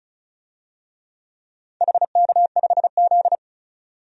Morse code, heard as a keyed tone, about 36 wpm, 700 Hz. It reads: HK5Z
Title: DXCC-100 Call Signs - 4 Characters - 35wpm